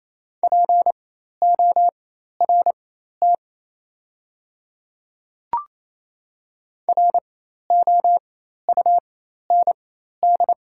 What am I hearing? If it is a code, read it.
Morse code, heard as a keyed tone, about 28 wpm, spaced as for 13 wpm, 690 Hz. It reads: PORT ROUND
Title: 4x Character Spacing - Top 200 Words - 28wpm